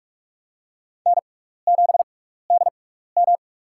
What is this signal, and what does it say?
Morse code, heard as a keyed tone, about 46 wpm, spaced as for 16 wpm, 700 Hz. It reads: N7DM
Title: USA Call Signs - 4 Characters - 6x Character Spacing - 45wpm